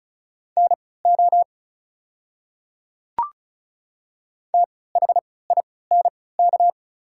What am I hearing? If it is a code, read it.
Morse code, heard as a keyed tone, about 34 wpm, spaced as for 20 wpm, 700 Hz. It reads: NO THINK
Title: 3x Character Spacing - Top 200 Words - 35wpm